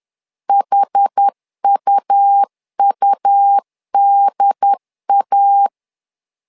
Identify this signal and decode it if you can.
Morse code, heard as a keyed tone, about 10 wpm, 800 Hz. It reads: HUUDA
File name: LEZIONI / 2 UHDN / 2 UHD 10wpm 1826